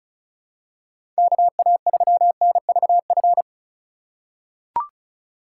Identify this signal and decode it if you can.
Morse code, heard as a keyed tone, about 36 wpm, 700 Hz. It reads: KA3NVF
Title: USA Call Signs - 6 Characters - 2x Character Spacing - 35wpm